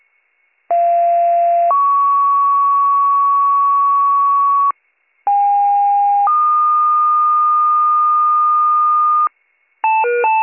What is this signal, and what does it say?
Two-tone page, heard as a one-second tone, then a three-second tone, about 690 then 1120 Hz, about 810 then 1220 Hz. It is heard from 0.7 to 9.3 s.